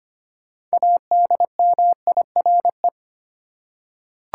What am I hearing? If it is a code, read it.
Morse code, heard as a keyed tone, about 25 wpm, 700 Hz. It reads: ADMIRE
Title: Six Letter Words - 25wpm